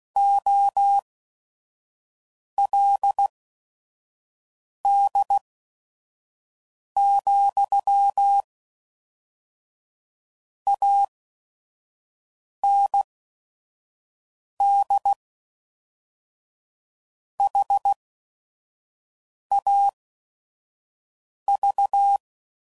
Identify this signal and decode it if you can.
Morse code, heard as a keyed tone, about 16 wpm, spaced as for 5 wpm, 800 Hz. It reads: OLD, AND HAV
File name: CW Code Groups @5WPM / QSOs / kc7sac